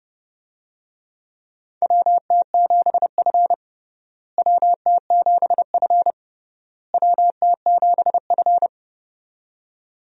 Morse code, heard as a keyed tone, about 30 wpm, 700 Hz. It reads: WT7F WT7F WT7F
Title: USA Call Signs - 4 Characters - Repeated 3x - 30wpm